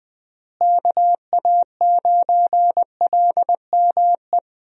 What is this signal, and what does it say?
Morse code, heard as a keyed tone, about 20 wpm, 700 Hz. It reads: KA9LME
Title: USA Call Signs - 6 Characters - Repeated 3x - 20wpm